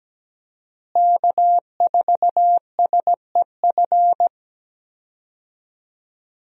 Morse code, heard as a keyed tone, about 17 wpm, 700 Hz. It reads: K4SEF
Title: USA Call Signs - 5 Characters - Repeated 2x - 17wpm